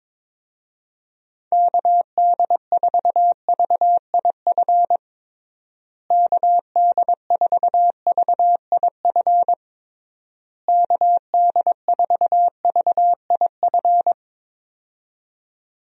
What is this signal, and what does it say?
Morse code, heard as a keyed tone, about 22 wpm, 700 Hz. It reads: KD4VIF KD4VIF KD4VIF
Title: USA Call Signs - 6 Characters - Repeated 3x - 22wpm